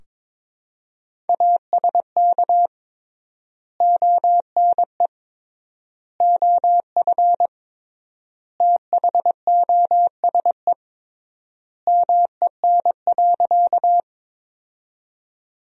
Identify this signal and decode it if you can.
Morse code, heard as a keyed tone, about 22 wpm, 700 Hz. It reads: ASK ONE OF THOSE MEN.
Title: 3x Word-Spacing - Sentences from Top 300 Words - 22wpm